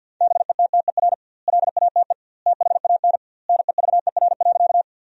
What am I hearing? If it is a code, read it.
Morse code, heard as a keyed tone, about 50 wpm, 700 Hz. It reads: BETTER LATE THAN NEVER.
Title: 2x Word-Spacing - Sentences from Top 400 Words - 50wpm